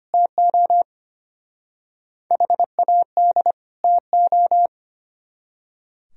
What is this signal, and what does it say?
Morse code, heard as a keyed tone, about 31 wpm, 700 Hz. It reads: TO HAD TO
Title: Speed-Racing - Sets of 2 Words 25wpm